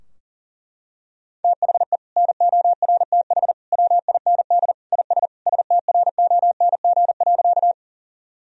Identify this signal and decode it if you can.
Morse code, heard as a keyed tone, about 40 wpm, 700 Hz. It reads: THE NORTH WIND IS STRONG.
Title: Sentences from Top 1000 Words - Review Part 2 - 40wpm